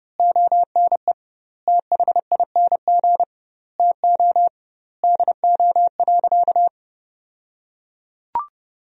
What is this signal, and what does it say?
Morse code, heard as a keyed tone, about 30 wpm, 700 Hz. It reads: ONE THING TO DO.
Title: 2x Word-Spacing - Sentences from Top 100 Words - 30wpm